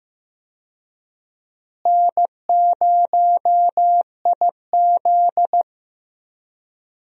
Morse code, heard as a keyed tone, about 15 wpm, 700 Hz. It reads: N0IZ
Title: USA Call Signs - 4 Characters - Repeated 3x - 15wpm